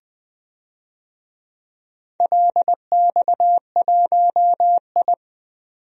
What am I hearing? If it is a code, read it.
Morse code, heard as a keyed tone, about 20 wpm, 700 Hz. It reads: LX1I
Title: DXCC-100 Call Signs - 4 Characters - 20wpm